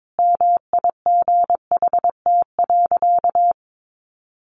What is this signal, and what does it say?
Morse code, heard as a keyed tone, about 22 wpm, 700 Hz. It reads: MIGHT.
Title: Sentences from Top 500 Words 22wpm